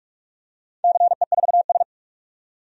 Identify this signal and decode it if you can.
Morse code, heard as a keyed tone, about 46 wpm, 700 Hz. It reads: CE4S
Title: DXCC-100 Call Signs - 4 Characters - 45wpm